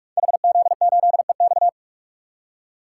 Morse code, heard as a keyed tone, about 44 wpm, 700 Hz. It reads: HZ8EX
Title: Code Groups - 5 Letter:Numbers - 45wpm